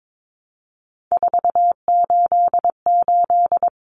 Morse code, heard as a keyed tone, about 22 wpm, 700 Hz. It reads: 488
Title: Numbers - 3-Digit 22wpm